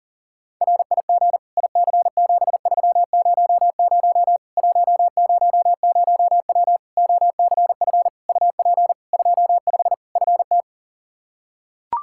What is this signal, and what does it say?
Morse code, heard as a keyed tone, about 37 wpm, 700 Hz. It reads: RIG IC7300 100W OCF UP 25 FT
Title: Practice QSO - Exchange 2 - Info - 40wpm